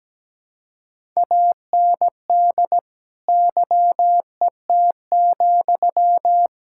Morse code, heard as a keyed tone, about 17 wpm, 700 Hz. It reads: AND YET,
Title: Sentences from Top 1600 Words 17wpm